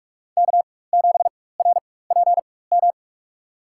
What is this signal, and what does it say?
Morse code, heard as a keyed tone, about 42 wpm, spaced as for 20 wpm, 700 Hz. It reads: K7RPM
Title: USA Call Signs - 5 Characters - 4x Character Spacing - 45wpm